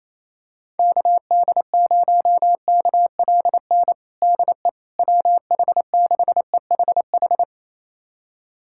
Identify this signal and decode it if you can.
Morse code, heard as a keyed tone, about 28 wpm, 700 Hz. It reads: KD0KLN DE WH6EHH
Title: USA Callsign DE Callsign - 6 Characters - 28wpm